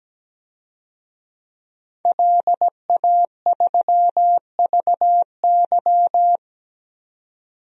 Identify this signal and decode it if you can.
Morse code, heard as a keyed tone, about 17 wpm, 700 Hz. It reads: LA3VY
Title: DXCC-100 Call Signs - 5 Characters - 17wpm